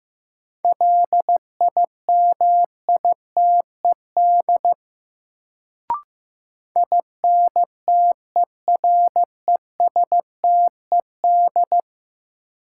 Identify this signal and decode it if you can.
Morse code, heard as a keyed tone, about 15 wpm, 700 Hz. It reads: LIMITED INTERESTED
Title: Sentences from Top 900 Words - New Word Drill - 15wpm